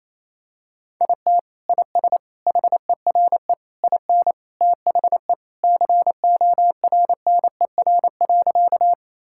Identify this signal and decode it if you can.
Morse code, heard as a keyed tone, about 28 wpm, 700 Hz. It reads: IT IS HERE IN THE CORNER.
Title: Sentences from Top 1600 Words 28wpm